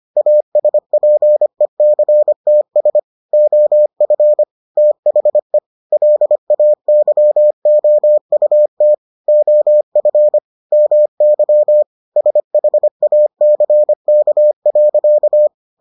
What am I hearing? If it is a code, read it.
Morse code, heard as a keyed tone, about 25 wpm, 600 Hz. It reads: ASPECTS OF THE LAYOUT OF MY SHACK.